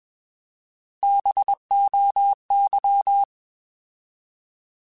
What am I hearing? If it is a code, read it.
Morse code, heard as a keyed tone, about 21 wpm, 800 Hz. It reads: BOY
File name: Alphabetical / Alphabetical mp3s / Bs @20